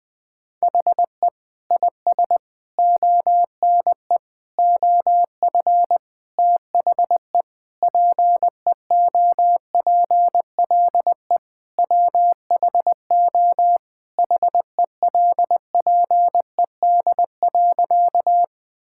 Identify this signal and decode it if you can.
Morse code, heard as a keyed tone, about 20 wpm, 700 Hz. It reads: HE IS ONE OF THE PEOPLE WHO HELPED.
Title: Sentences with 4-Word Phrases - 20wpm